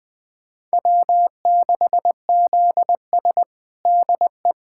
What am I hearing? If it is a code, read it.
Morse code, heard as a keyed tone, about 21 wpm, 700 Hz. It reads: W6ZS DE
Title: USA Callsign DE Callsign - 4 Characters - 20wpm